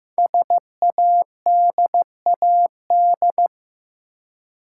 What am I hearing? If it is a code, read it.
Morse code, heard as a keyed tone, about 15 wpm, 700 Hz. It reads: SADAD
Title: Code Groups - 5 Letters - 15wpm